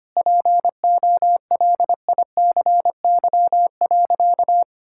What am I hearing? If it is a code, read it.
Morse code, heard as a keyed tone, about 25 wpm, 700 Hz. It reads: POLICY.
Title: Sentences from Top 1000 Words - Review Part 5 - 25wpm